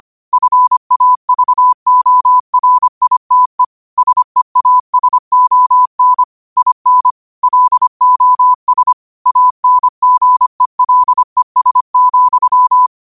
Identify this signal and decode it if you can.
Morse code, heard as a keyed tone, about 25 wpm, 1000 Hz. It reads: FAVORITE SEASON IN LOS ANGELES,